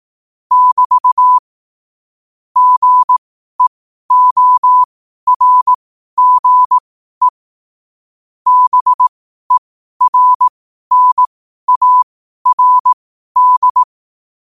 Morse code, heard as a keyed tone, about 18 wpm, spaced as for 12 wpm, 1000 Hz. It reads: <BT> GEORGE BERNARD
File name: qotdfiles / 12wpm